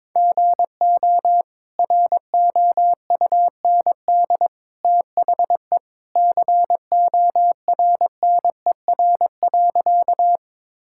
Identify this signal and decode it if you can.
Morse code, heard as a keyed tone, about 22 wpm, 700 Hz. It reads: GO ROUND THE CORNER.
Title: Sentences from Top 1600 Words 22wpm